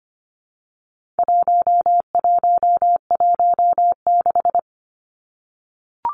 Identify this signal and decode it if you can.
Morse code, heard as a keyed tone, about 25 wpm, 700 Hz. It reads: 1116
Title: Numbers - 4-Digit 25wpm